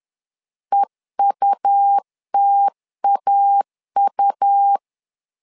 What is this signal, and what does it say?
Morse code, heard as a keyed tone, about 10 wpm, 800 Hz. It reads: EUTAU
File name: LEZIONI / 2 UHDN / 2 UH 10wpm 1714